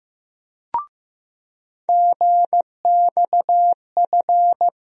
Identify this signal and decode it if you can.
Morse code, heard as a keyed tone, about 15 wpm, 700 Hz. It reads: GXF
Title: USA Call Sign Postfixes - 3 Letters 15wpm